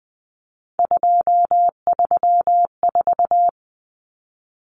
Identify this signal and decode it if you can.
Morse code, heard as a keyed tone, about 20 wpm, 700 Hz. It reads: 234
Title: Numbers - 3-Digit 20wpm